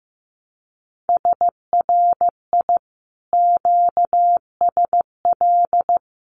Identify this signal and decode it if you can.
Morse code, heard as a keyed tone, about 15 wpm, 700 Hz. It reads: SRI QSL
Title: Common QSO Phrases Encore 15wpm